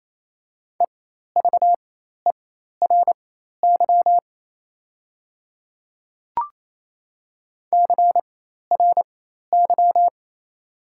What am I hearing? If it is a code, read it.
Morse code, heard as a keyed tone, about 28 wpm, spaced as for 13 wpm, 690 Hz. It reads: EVERY CRY
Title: 4x Character Spacing - Top 500 Words - 28wpm